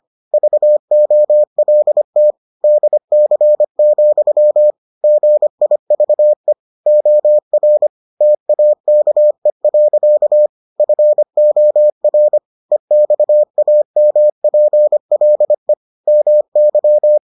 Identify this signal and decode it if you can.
Morse code, heard as a keyed tone, about 25 wpm, 600 Hz. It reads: VOLT DC, GIVE OR TAKE. FOR EXAMPLE MY